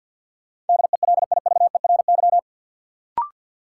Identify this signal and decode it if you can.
Morse code, heard as a keyed tone, about 53 wpm, 700 Hz. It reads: DELIVERY
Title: Sentences from Top 1500 Words - New Word Drill - 50wpm